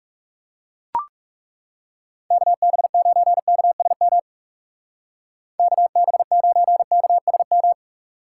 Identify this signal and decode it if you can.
Morse code, heard as a keyed tone, about 36 wpm, 700 Hz. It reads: KB9KSM KB9KSM
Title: Speed Racing - USA 2x3 Call Signs 30wpm